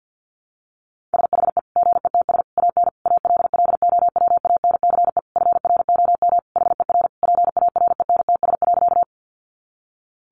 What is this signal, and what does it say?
Morse code, heard as a keyed tone, about 48 wpm, 700 Hz. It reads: SHE GETS AN ALLOWANCE FROM HER PARENTS.
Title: Easy Phrases 50wpm